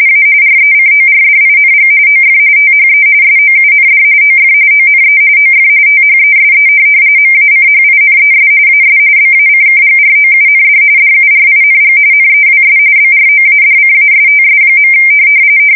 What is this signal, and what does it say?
RTTY, about 45.45 baud. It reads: A RECORDING OF THE SOUND OF ONE OF THESE MACHINES AS BACKGROUND DURING NEWS BROADCASTS.
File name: fsk170 / teletype-corp